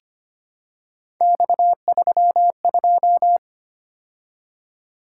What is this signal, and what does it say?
Morse code, heard as a keyed tone, about 25 wpm, 700 Hz. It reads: X32
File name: Three Letter-Number Combinations - 25wpm